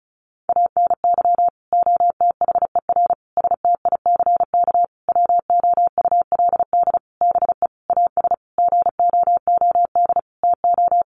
Morse code, heard as a keyed tone, about 34 wpm, 700 Hz. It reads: ANY OTHER STICK WOULD BE AS GOOD TO
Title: Sentences from Top 1200 Words 35wpm